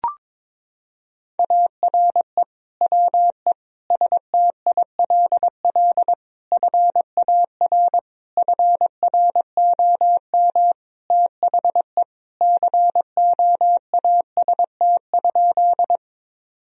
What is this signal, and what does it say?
Morse code, heard as a keyed tone, about 22 wpm, 700 Hz. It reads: ARE WE STILL FAR FROM THE COAST?
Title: Sentences from Top 1000 Words - Review Part 4 - 22wpm